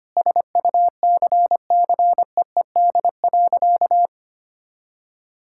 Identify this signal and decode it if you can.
Morse code, heard as a keyed tone, about 25 wpm, 700 Hz. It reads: SUCCEED.
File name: practice / 25wpm / Sentences with 4-Word Phrases - 25wpm